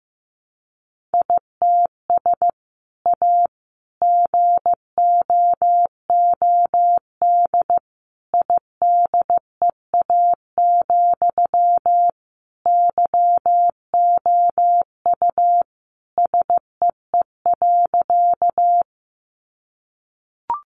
Morse code, heard as a keyed tone, about 14 wpm, 700 Hz. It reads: ITS A GOOD IDEA, YOU SEE.
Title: Sentences from Top 600 Words 15wpm